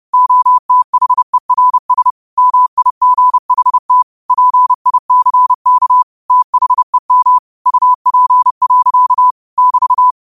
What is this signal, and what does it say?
Morse code, heard as a keyed tone, about 30 wpm, 1000 Hz. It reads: OTHERS MIGHT PICK THEM UP. <BT>